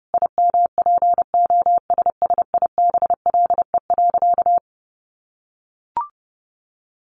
Morse code, heard as a keyed tone, about 28 wpm, 700 Hz. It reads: IMPOSSIBLE.
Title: Sentences from Top 1100 Words 30wpm